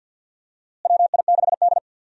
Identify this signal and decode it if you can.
Morse code, heard as a keyed tone, about 53 wpm, 700 Hz. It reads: WI6D
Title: USA Call Signs - 4 Characters - Repeated 3x - 50wpm